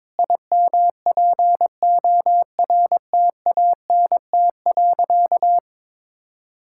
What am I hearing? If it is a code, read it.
Morse code, heard as a keyed tone, about 22 wpm, 700 Hz. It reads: IMPORTANT.
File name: Sentences with 7-Word Phrases - 22wpm